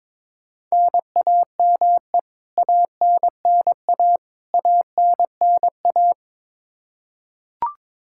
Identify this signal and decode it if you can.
Morse code, heard as a keyed tone, about 22 wpm, 700 Hz. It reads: NAME ANNA ANNA
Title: Practice QSO - Exchange 1 - Beginning - 22wpm